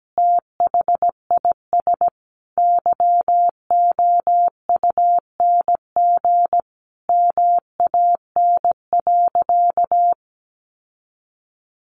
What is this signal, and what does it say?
Morse code, heard as a keyed tone, about 17 wpm, 700 Hz. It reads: THIS YOUNG MAN.
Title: Sentences from Top 1300 Words 17wpm